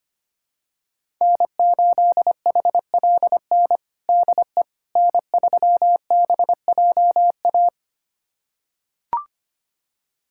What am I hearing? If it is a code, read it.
Morse code, heard as a keyed tone, about 25 wpm, 700 Hz. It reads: N8HLN DE N3BJA
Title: USA Callsign DE Callsign - 5 Characters - 25wpm